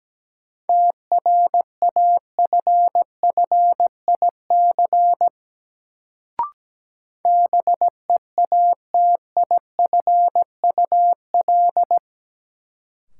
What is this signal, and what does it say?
Morse code, heard as a keyed tone, about 17 wpm, 700 Hz. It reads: TRAFFIC BEATIFUL
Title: Sentences from Top 1300 Words - New Word Drill - 17wpm